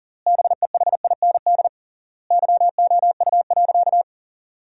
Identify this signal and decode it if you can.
Morse code, heard as a keyed tone, about 37 wpm, 700 Hz. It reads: BEHIND YOU.
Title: 3x Word-Spacing - Sentences from Top 500 Words - 40wpm